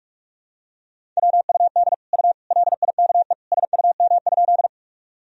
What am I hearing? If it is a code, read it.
Morse code, heard as a keyed tone, about 45 wpm, 700 Hz. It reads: WUD U LIKE SUM?
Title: Sentences from Top 100 Words and Abbreviations 45wpm